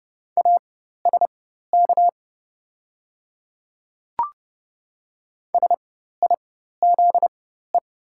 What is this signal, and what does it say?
Morse code, heard as a keyed tone, about 31 wpm, spaced as for 14 wpm, 700 Hz. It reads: ASK SIZE
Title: 4x Character Spacing - Top 500 Words - 30wpm